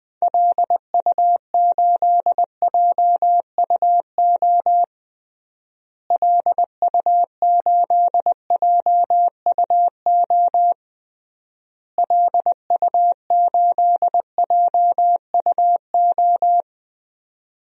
Morse code, heard as a keyed tone, about 20 wpm, 700 Hz. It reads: LU8JUO LU8JUO LU8JUO
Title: DXCC-100 Call Signs - 6 Characters - Repeated 3x - 20wpm